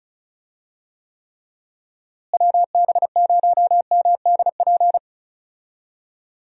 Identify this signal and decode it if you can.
Morse code, heard as a keyed tone, about 34 wpm, 700 Hz. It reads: WB0MDP
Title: USA Call Signs - 6 Characters - Repeated 3x - 35wpm